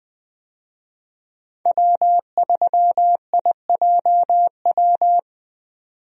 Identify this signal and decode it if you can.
Morse code, heard as a keyed tone, about 20 wpm, 700 Hz. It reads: W3IJW
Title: USA Call Signs - 5 Characters - 20wpm